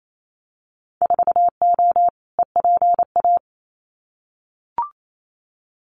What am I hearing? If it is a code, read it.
Morse code, heard as a keyed tone, about 28 wpm, 700 Hz. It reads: 4O EPA
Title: Winter Field Day Exchanges 28wpm